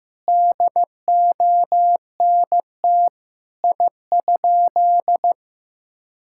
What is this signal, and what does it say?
Morse code, heard as a keyed tone, about 15 wpm, 700 Hz. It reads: DONT I?
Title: Sentences from Top 1000 Words - Review Part 5 - 15wpm